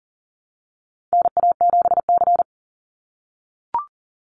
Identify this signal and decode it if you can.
Morse code, heard as a keyed tone, about 38 wpm, 700 Hz. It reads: NA7C
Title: Call Signs - 40wpm